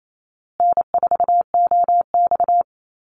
Morse code, heard as a keyed tone, about 28 wpm, 700 Hz. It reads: N4OX
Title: Easy Call Signs Repeated 3x - 28wpm